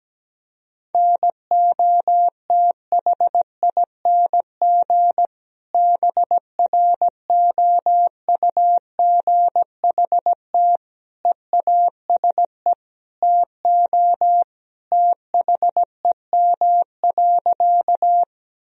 Morse code, heard as a keyed tone, about 17 wpm, 700 Hz. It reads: NOTHING BROUGHT EASE TO THEM.
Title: Sentences from Top 1000 Words - Review Part 1 - 17wpm